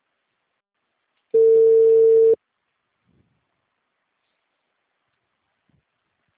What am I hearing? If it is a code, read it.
Ringback tone (China, 450 Hz).
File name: out- -2014- - - .1917